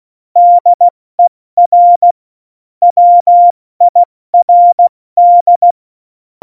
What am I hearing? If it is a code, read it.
Morse code, heard as a keyed tone, about 16 wpm, 700 Hz. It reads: DER WIRD
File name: Morsekurs / Witze / Witz 04. Juni 2019, 70 F80